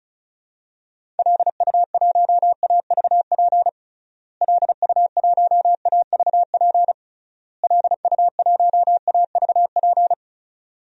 Morse code, heard as a keyed tone, about 36 wpm, 700 Hz. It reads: LU1AVP LU1AVP LU1AVP
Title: DXCC-100 Call Signs - 6 Characters - Repeated 3x - 35wpm